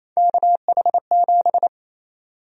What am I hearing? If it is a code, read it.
Morse code, heard as a keyed tone, about 28 wpm, 700 Hz. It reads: KH7
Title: USA Call Sign Prefixes - 28wpm